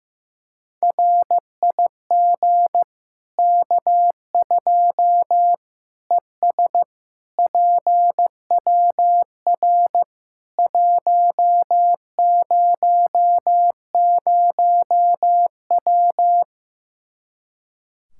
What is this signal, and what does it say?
Morse code, heard as a keyed tone, about 15 wpm, 700 Hz. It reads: RIG K2 ES PWR 100W
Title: K1USN SST QSO Practice 15wpm